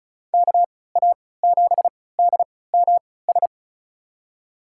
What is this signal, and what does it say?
Morse code, heard as a keyed tone, about 36 wpm, spaced as for 20 wpm, 700 Hz. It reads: KA7DMS
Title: USA Call Signs - 6 Characters - 3x Character Spacing - 35wpm